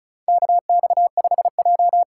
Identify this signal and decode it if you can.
Morse code, heard as a keyed tone, about 36 wpm, 700 Hz. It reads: KX5J
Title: USA Call Signs - 4 Characters - 2x Character Spacing - 35wpm